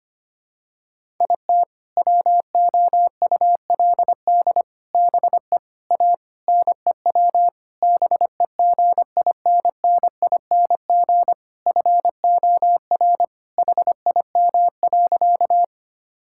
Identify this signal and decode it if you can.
Morse code, heard as a keyed tone, about 25 wpm, 700 Hz. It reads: IT WOULD BE A NEW BEGINNING FOR HIM.
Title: Sentences from Top 1000 Words - Review Part 5 - 25wpm